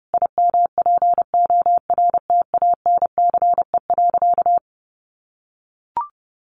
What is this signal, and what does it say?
Morse code, heard as a keyed tone, about 30 wpm, 700 Hz. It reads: IMPORTANCE.
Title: Sentences from Top 1200 Words 30wpm